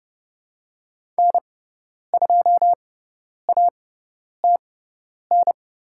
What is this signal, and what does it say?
Morse code, heard as a keyed tone, about 30 wpm, spaced as for 10 wpm, 700 Hz. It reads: N2ATN
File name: Lesson 18g - Letter m - Call Signs 30wpm